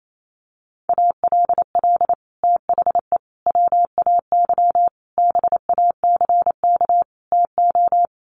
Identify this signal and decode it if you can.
Morse code, heard as a keyed tone, about 28 wpm, 690 Hz. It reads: ALL THE WAY BACK TO
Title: Sets of 5 Words Encore 28wpm